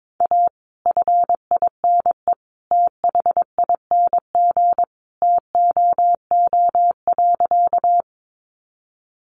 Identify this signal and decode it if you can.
Morse code, heard as a keyed tone, about 22 wpm, 700 Hz. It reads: A FINE THING TOO.
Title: Sentences from Top 500 Words 22wpm